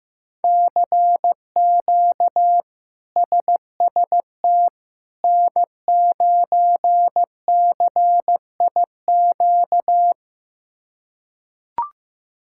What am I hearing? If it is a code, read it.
Morse code, heard as a keyed tone, about 15 wpm, 700 Hz. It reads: CQ SST N9CIQ
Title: K1USN SST QSO Practice - 15wpm